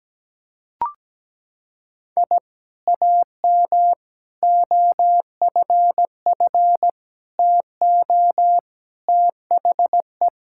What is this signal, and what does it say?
Morse code, heard as a keyed tone, about 17 wpm, 700 Hz. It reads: I AM OFF TO THE
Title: Sentences from Top 1000 Words - Review Part 5 - 17wpm